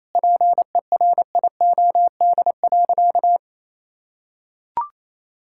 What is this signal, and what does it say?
Morse code, heard as a keyed tone, about 28 wpm, 700 Hz. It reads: PERIOD.
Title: Sentences from Top 1000 Words - Review Part 5 - 28wpm